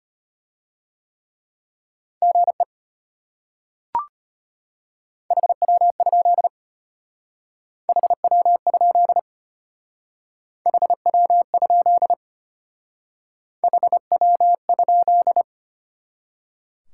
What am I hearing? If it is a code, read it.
Morse code, heard as a keyed tone, about 32 wpm, 700 Hz. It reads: GE HW? HW? HW? HW?